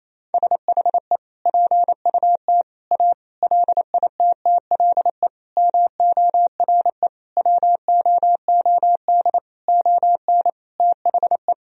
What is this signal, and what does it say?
Morse code, heard as a keyed tone, about 27 wpm, 700 Hz. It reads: SHE PUT A LITTLE MORE WOOD ON THE
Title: Sentences from Top 1000 Words - Review Part 2 - 28wpm